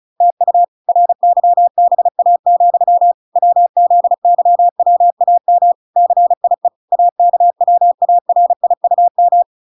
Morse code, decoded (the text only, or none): TU RYBA, WZYWAM CIE AKWARIUM